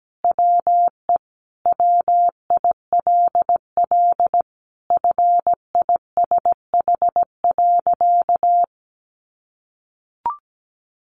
Morse code, heard as a keyed tone, about 17 wpm, 700 Hz. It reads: WE WILL FISH.
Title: Sentences from Top 1100 Words 17wpm